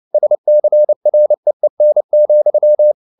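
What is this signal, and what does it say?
Morse code, heard as a keyed tone, about 29 wpm, 600 Hz. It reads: SCREEN,
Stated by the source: Morse code at 29 wpm, sending Ken Follett's novel Never in English, from CW med Gnister.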